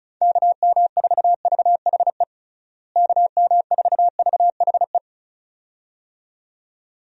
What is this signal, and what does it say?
Morse code, heard as a keyed tone, about 36 wpm, 700 Hz. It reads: KM4VHE KM4VHE
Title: USA Call Signs - 6 Characters - Repeated 2x - 35wpm